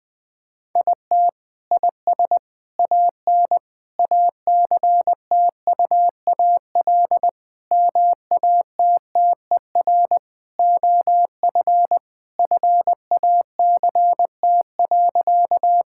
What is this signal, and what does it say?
Morse code, heard as a keyed tone, about 20 wpm, 700 Hz. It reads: IT IS AN ACTUAL MATTER OF FACT.
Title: Sentences from Top 1000 Words - Review Part 5 - 20wpm